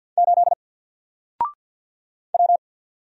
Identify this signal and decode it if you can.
Morse code, heard as a keyed tone, about 48 wpm, 700 Hz. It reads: 8 W
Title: Single Letter-Number Alt 50wpm